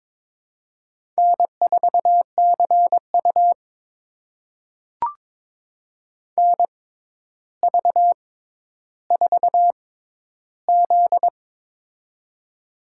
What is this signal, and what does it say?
Morse code, heard as a keyed tone, about 22 wpm, spaced as for 11 wpm, 700 Hz. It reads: N4CU NV4Z
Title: USA Call Signs - 4 Characters - 6x Character Spacing - 22wpm